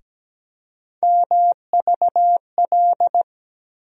Morse code, heard as a keyed tone, about 17 wpm, 700 Hz. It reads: MVL
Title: USA Call Sign Postfixes - 3 Letters 17wpm